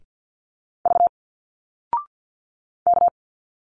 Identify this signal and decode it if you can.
Morse code, heard as a keyed tone, about 49 wpm, 700 Hz. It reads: V K
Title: Single Letters 50wpm